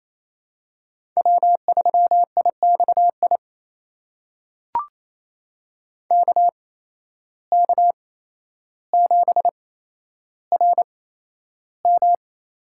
Morse code, heard as a keyed tone, about 28 wpm, spaced as for 12 wpm, 700 Hz. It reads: W3IXI KK7RM
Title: USA Call Signs - 5 Characters - 8x Character Spacing - 28wpm